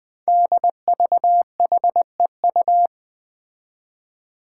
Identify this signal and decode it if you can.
Morse code, heard as a keyed tone, about 21 wpm, 700 Hz. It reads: DVHEU
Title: Five Challenging Letter-Number Combinations 20wpm